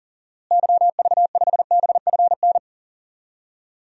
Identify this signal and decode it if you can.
Morse code, heard as a keyed tone, about 40 wpm, 700 Hz. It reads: YV5BFN
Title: DXCC-100 Call Signs - 6 Characters - Repeated 2x - 40wpm